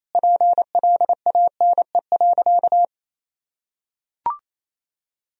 Morse code, heard as a keyed tone, about 28 wpm, 700 Hz. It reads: PLANE.
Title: Sentences from Top 1000 Words - Review Part 3 - 28wpm